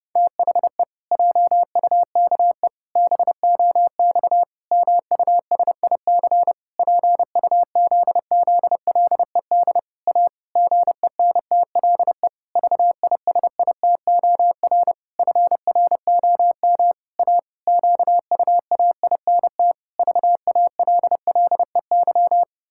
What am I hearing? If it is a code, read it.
Morse code, heard as a keyed tone, about 30 wpm, 700 Hz. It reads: THE JUKE BOX MUSIC PUZZLED A GENTLE VISITOR FROM A QUAINT VALLEY
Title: Pangram Challenge v2 - 30wpm